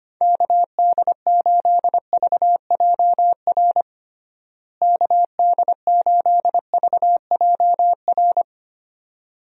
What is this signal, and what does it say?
Morse code, heard as a keyed tone, about 25 wpm, 700 Hz. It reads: KD8VJR KD8VJR
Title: USA Call Signs - 6 Characters - Repeated 2x - 25wpm